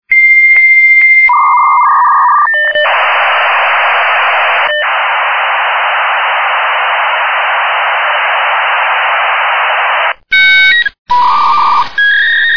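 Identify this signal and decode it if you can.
Dial-up modem handshake.
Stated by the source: Dial-up modem sounds made into a mobile phone ringtone.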